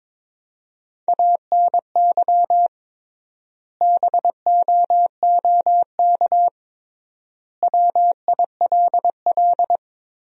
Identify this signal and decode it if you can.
Morse code, heard as a keyed tone, about 22 wpm, 700 Hz. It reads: ANY BOOK WILL
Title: 3x Word-Spacing - Sentences from Top 300 Words - 22wpm